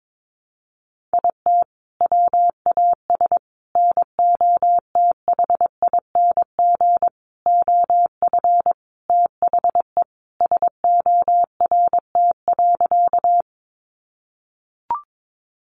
Morse code, heard as a keyed tone, about 22 wpm, 700 Hz. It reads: IT WAS NOTHING OF THE SORT.
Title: Sentences from Top 1300 Words 22wpm